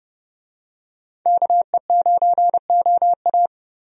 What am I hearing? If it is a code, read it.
Morse code, heard as a keyed tone, about 30 wpm, 700 Hz. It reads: KE9OA
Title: USA Call Signs - 5 Characters - Repeated 3x - 30wpm